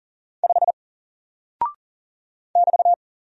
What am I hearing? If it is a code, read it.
Morse code, heard as a keyed tone, about 40 wpm, 700 Hz. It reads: 5 <BT>
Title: Single Character (Letter:Number:Punctuation:Prosign) Alt 40wpm